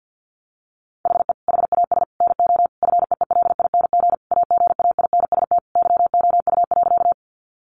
Morse code, heard as a keyed tone, about 50 wpm, 700 Hz. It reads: HE HAS NO FEELING AGAINST YOU.
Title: Sentences from Top 1200 Words 50wpm